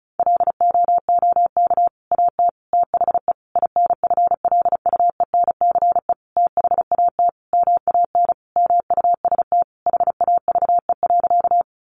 Morse code, heard as a keyed tone, about 34 wpm, 700 Hz. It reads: LOOK AT THE INFLUENCE THAT MAN MUST HAVE.
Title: Sentences from Top 1100 Words 35wpm